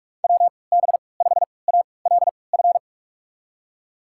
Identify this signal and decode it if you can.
Morse code, heard as a keyed tone, about 45 wpm, spaced as for 25 wpm, 700 Hz. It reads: WB5ALF
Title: USA Call Signs - 6 Characters - 3x Character Spacing - 45wpm